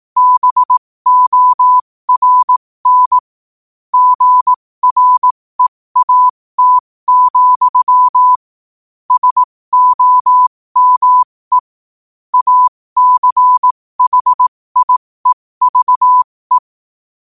Morse code, heard as a keyed tone, about 18 wpm, spaced as for 15 wpm, 1000 Hz. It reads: BORN GREAT, SOME ACHIEVE
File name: qotdfiles / 15wpm